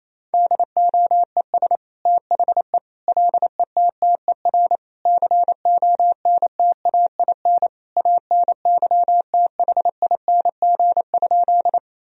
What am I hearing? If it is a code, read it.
Morse code, heard as a keyed tone, about 28 wpm, 700 Hz. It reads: DOES THE LETTER CONTAIN ANYTHING?
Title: Sentences from Top 1000 Words - Review Part 2 - 28wpm